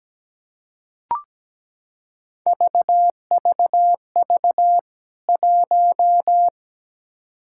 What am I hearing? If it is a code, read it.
Morse code, heard as a keyed tone, about 17 wpm, 700 Hz. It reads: VVV 1
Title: K1USN SST QSO Practice 17wpm